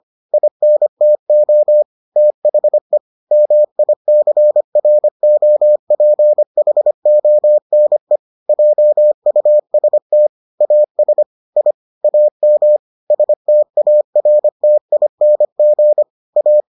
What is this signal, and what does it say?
Morse code, heard as a keyed tone, about 25 wpm, 600 Hz. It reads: INTO THE MICROPHONE JUST AS I AM STARTING A